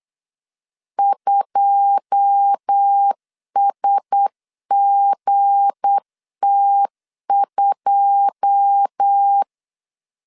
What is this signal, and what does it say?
Morse code, heard as a keyed tone, about 8 wpm, 800 Hz. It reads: 2SGT2
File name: LEZIONI / 9 2 8wpm 3437